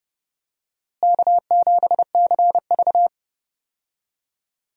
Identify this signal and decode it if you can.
Morse code, heard as a keyed tone, about 30 wpm, 700 Hz. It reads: K7CV
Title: USA Call Signs - 4 Characters - Repeated 3x - 30wpm